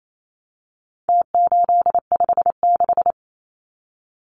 Morse code, heard as a keyed tone, about 28 wpm, 700 Hz. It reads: T856
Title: Numbers - Contest Style - 4-Digit 28wpm